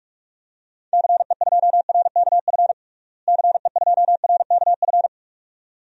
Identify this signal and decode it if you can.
Morse code, heard as a keyed tone, about 43 wpm, 700 Hz. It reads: CE2RKF CE2RKF
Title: DXCC-100 Call Signs - 6 Characters - Repeated 2x - 45wpm